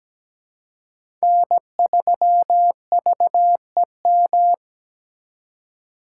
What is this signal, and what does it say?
Morse code, heard as a keyed tone, about 20 wpm, 700 Hz. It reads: N3VEM
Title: Speed Racing - USA 1x3 Call Signs 17wpm